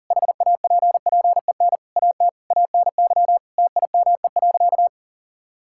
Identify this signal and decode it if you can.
Morse code, heard as a keyed tone, about 40 wpm, 700 Hz. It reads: HAPPEN AT ANY TIME.
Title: Sentences from Top 1000 Words - Review Part 6 - 40wpm